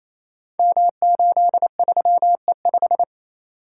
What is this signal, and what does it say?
Morse code, heard as a keyed tone, about 28 wpm, 700 Hz. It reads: M83E5
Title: Code Groups - 5 Letter:Numbers - 28wpm